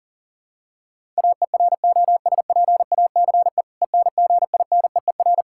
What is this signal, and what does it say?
Morse code, heard as a keyed tone, about 40 wpm, 700 Hz. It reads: AEROSPACE ENGINEER
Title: Practice QSO - Exchange 3 - More Info - 40wpm